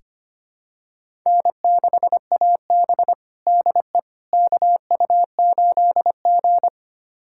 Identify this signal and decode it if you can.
Morse code, heard as a keyed tone, about 25 wpm, 700 Hz. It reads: N6AB DE KU8G
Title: USA Callsign DE Callsign - 4 Characters - 25wpm